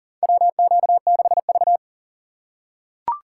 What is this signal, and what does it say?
Morse code, heard as a keyed tone, about 40 wpm, 700 Hz. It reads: WQ6V
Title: USA Call Signs - 4 Characters - 6x Character Spacing - 40wpm